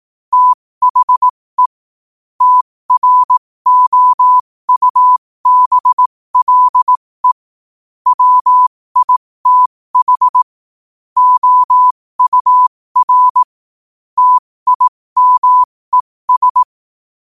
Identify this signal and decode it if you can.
Morse code, heard as a keyed tone, about 18 wpm, spaced as for 15 wpm, 1000 Hz. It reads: THE TROUBLE WITH OUR TIMES
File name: qotdfiles / 15wpm